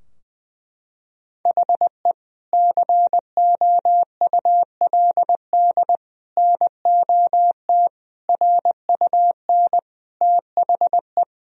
Morse code, heard as a keyed tone, about 20 wpm, 700 Hz. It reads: HE COULD NOT RUN THE
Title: Sentences from Top 1000 Words - Review Part 2 - 20wpm